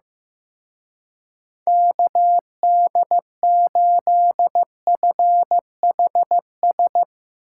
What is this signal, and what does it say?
Morse code, heard as a keyed tone, about 15 wpm, 700 Hz. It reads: KD8FHS
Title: Speed Racing - USA Call Signs 15wpm